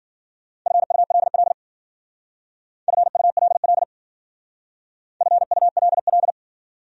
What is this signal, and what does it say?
Morse code, heard as a keyed tone, about 52 wpm, 700 Hz. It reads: FULL FULL FULL